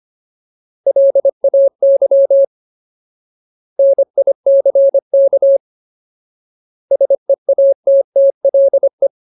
Morse code, heard as a keyed tone, about 25 wpm, 550 Hz. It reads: LAY NICK SEATTLE
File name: Ultimate Word List / 25WPM Ultimate